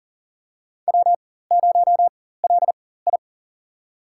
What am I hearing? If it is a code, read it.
Morse code, heard as a keyed tone, about 40 wpm, spaced as for 19 wpm, 690 Hz. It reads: W0LI
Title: USA Call Signs - 4x Character Spacing - 40wpm